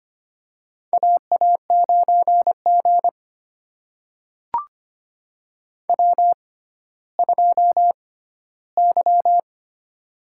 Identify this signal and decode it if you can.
Morse code, heard as a keyed tone, about 25 wpm, spaced as for 14 wpm, 700 Hz. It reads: AA9G W2Y
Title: USA Call Signs - 6x Character Spacing - 25wpm